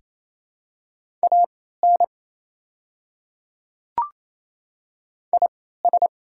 Morse code, heard as a keyed tone, about 28 wpm, spaced as for 16 wpm, 700 Hz. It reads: AN IS
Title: 3x Character Spacing - Top 100 Words - 28wpm